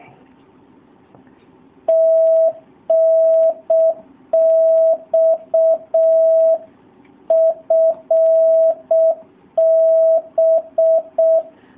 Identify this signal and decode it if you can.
Morse code, heard as a keyed tone, about 6 wpm, 650 Hz. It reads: TNX FB
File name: jh8cbh / 2019cw / SP07- w